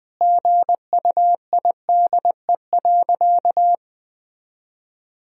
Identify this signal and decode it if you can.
Morse code, heard as a keyed tone, about 21 wpm, 700 Hz. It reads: GUIDE.
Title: Sentences from Top 1000 Words - Review Part 3 - 20wpm